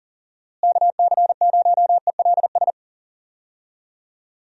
Morse code, heard as a keyed tone, about 40 wpm, 700 Hz. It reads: KC0ELS
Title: USA Call Signs - 6 Characters - Repeated 3x - 40wpm